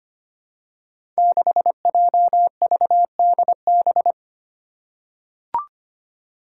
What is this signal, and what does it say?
Morse code, heard as a keyed tone, about 25 wpm, 700 Hz. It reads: 6JVDB
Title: Five Challenging Letter-Number Combinations 25wpm